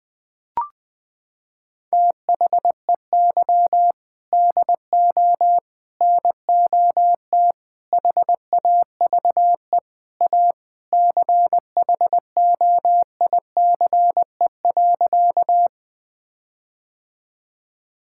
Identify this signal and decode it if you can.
Morse code, heard as a keyed tone, about 20 wpm, 700 Hz. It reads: THEY DO NOT HAVE A CHOICE.
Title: Sentences with 4-Word Phrases - 20wpm